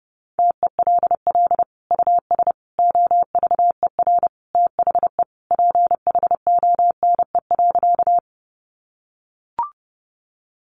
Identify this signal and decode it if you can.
Morse code, heard as a keyed tone, about 30 wpm, 700 Hz. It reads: TELL US OVER THE PHONE.
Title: Sentences from Top 1100 Words 30wpm